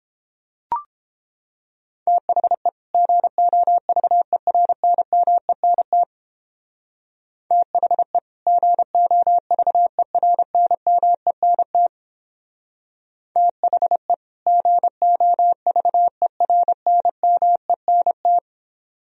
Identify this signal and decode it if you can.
Morse code, heard as a keyed tone, about 31 wpm, 700 Hz. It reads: THE GOVERNMENT THE GOVERNMENT THE GOVERNMENT